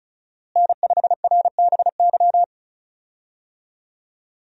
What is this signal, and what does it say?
Morse code, heard as a keyed tone, about 34 wpm, 700 Hz. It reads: N5RBY
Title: USA Call Signs - 5 Characters - Repeated 3x - 35wpm